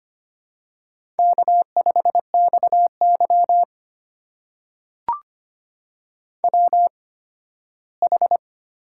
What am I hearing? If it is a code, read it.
Morse code, heard as a keyed tone, about 25 wpm, spaced as for 15 wpm, 700 Hz. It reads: K5XY WH
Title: USA Call Signs - 4 Characters - 8x Character Spacing - 25wpm